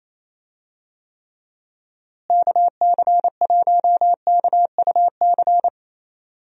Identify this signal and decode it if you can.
Morse code, heard as a keyed tone, about 28 wpm, 700 Hz. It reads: KC1KUC
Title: USA Call Signs - 6 Characters - Repeated 3x - 28wpm